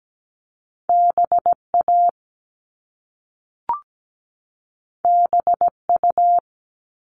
Morse code, heard as a keyed tone, about 17 wpm, 700 Hz. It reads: BA BU
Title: All Cut Numbers - 2-Digit 17wpm